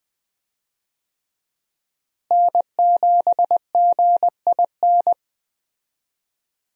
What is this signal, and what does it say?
Morse code, heard as a keyed tone, about 21 wpm, 700 Hz. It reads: N7GIN
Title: USA Call Signs - 5 Characters - Repeated 3x - 20wpm